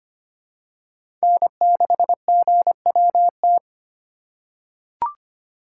Morse code, heard as a keyed tone, about 25 wpm, 700 Hz. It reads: N6GWT
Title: USA Call Signs - 5 Characters - 4x Character Spacing - 25wpm